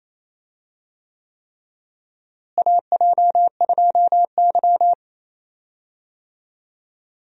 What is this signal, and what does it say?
Morse code, heard as a keyed tone, about 28 wpm, 700 Hz. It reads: AJ2Y
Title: USA Call Signs - 4 Characters - Repeated 3x - 28wpm